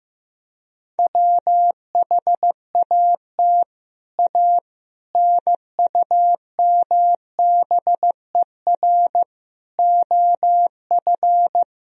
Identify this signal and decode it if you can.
Morse code, heard as a keyed tone, about 16 wpm, 700 Hz. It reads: WHAT A NUMBER OF
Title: Sentences from Top 1000 Words - Review Part 1 - 15wpm